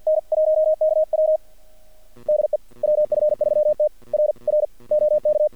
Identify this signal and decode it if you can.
Morse code, heard as a keyed tone, about 50 wpm, 620 Hz. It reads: N1KW DE WF3T R R OK
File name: archived / CRSnet